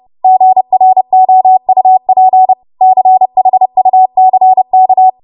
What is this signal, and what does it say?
Morse code, heard as a keyed tone, about 30 wpm, 750 Hz. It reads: GROUP CHUCK
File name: Morse / 30WPM